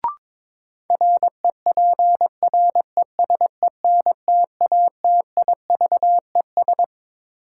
Morse code, heard as a keyed tone, about 22 wpm, 700 Hz. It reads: REPRESENTATIVES
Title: Sets of 5 Words Encore - New Word Drill - 22wpm